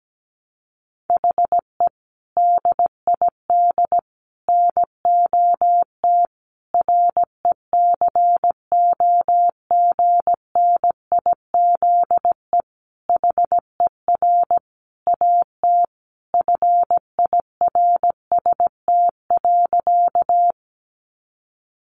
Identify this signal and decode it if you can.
Morse code, heard as a keyed tone, about 17 wpm, 700 Hz. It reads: HE DID NOT RECOGNIZE HER AT FIRST.
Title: Sentences from Top 1400 Words 17wpm